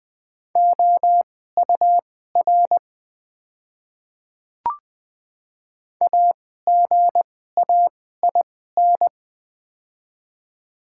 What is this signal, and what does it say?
Morse code, heard as a keyed tone, about 20 wpm, spaced as for 14 wpm, 700 Hz. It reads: OUR AGAIN
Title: 2x Character Spacing - Top 200 Words - 20wpm